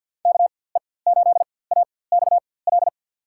Morse code, heard as a keyed tone, about 52 wpm, spaced as for 24 wpm, 700 Hz. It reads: KE8AXL
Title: USA Call Signs - 6 Characters - 4x Character Spacing - 50wpm